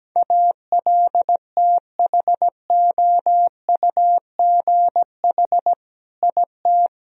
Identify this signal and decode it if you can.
Morse code, heard as a keyed tone, about 17 wpm, 700 Hz. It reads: ALTHOUGH IT
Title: Sentences from Top 1000 Words - Review Part 3 - 17wpm